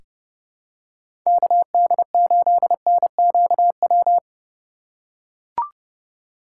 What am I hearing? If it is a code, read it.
Morse code, heard as a keyed tone, about 31 wpm, 700 Hz. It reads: KD8NQW
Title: USA Call Signs - 6 Characters - 2x Character Spacing - 30wpm